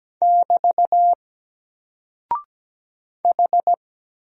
Morse code, heard as a keyed tone, about 17 wpm, 700 Hz. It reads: <BT> H
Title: Single Character (Letter:Number:Punctuation:Prosign) 17wpm